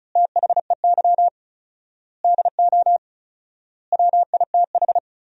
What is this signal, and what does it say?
Morse code, heard as a keyed tone, about 34 wpm, 700 Hz. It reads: THEY DO WITH
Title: 4x Word-Spacing - Sentences from Top 200 Words - 35wpm